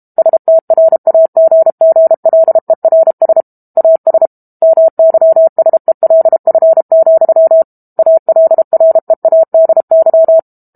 Morse code, heard as a keyed tone, about 32 wpm, 650 Hz. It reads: STRAGGLERS AS MYSELF, ALREADY
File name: Book Two / 06-Chapter 6 - 32 WPM